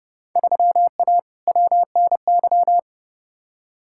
Morse code, heard as a keyed tone, about 32 wpm, 700 Hz. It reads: 3A WNY
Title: Field Day Exchanges - 30wpm